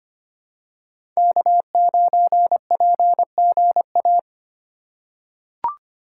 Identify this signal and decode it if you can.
Morse code, heard as a keyed tone, about 25 wpm, 700 Hz. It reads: K9PGA
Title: USA Call Signs - 5 Characters - 3x Character Spacing - 25wpm